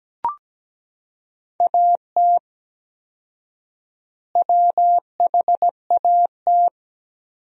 Morse code, heard as a keyed tone, about 17 wpm, 700 Hz. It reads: AT WHAT
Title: 4x Word-Spacing - Sentences from Top 100 Words - 17wpm